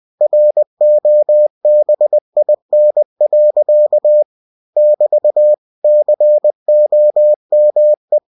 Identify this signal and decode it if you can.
Morse code, heard as a keyed tone, about 20 wpm, 600 Hz. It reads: ROBIN. <BT> COME
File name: cw-books / winnie-the-pooh / winnie-20wpm-page-0080